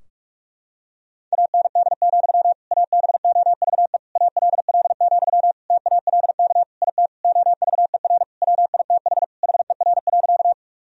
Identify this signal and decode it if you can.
Morse code, heard as a keyed tone, about 41 wpm, 700 Hz. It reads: AND, ABOVE ALL, TALK IT OVER WITH HER.
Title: Sentences from Top 1000 Words - Review Part 2 - 45wpm